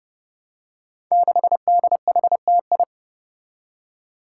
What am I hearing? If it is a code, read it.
Morse code, heard as a keyed tone, about 29 wpm, 700 Hz. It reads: 6DHTI
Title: Five Challenging Letter-Number Combinations 30wpm